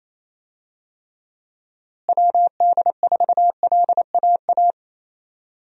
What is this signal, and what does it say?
Morse code, heard as a keyed tone, about 28 wpm, 700 Hz. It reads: WD4LAA
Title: USA Call Signs - 6 Characters - Repeated 3x - 28wpm